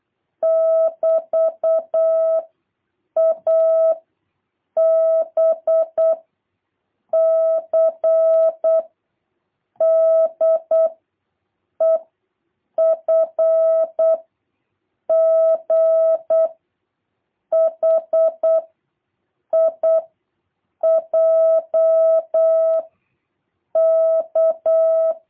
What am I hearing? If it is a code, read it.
Morse code, heard as a keyed tone, about 8 wpm, 650 Hz. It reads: <BT> A B C D E F G H I J K